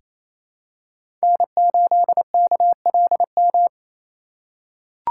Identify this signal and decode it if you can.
Morse code, heard as a keyed tone, about 27 wpm, 700 Hz. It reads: N8KLM
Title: USA Call Signs - 5 Characters - 4x Character Spacing - 28wpm